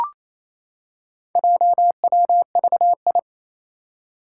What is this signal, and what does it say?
Morse code, heard as a keyed tone, about 28 wpm, 700 Hz. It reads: JWVI
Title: Four Challenging Letter-Number Combinations 28wpm